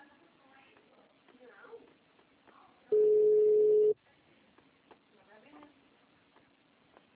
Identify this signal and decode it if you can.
Ringback tone (Europe, 425 Hz), with speech.
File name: out- -2006- - - .37